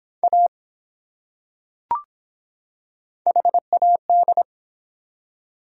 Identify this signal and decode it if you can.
Morse code, heard as a keyed tone, about 23 wpm, 700 Hz. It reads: A HAD